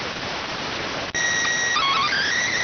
SSTV (slow-scan television), PD 290.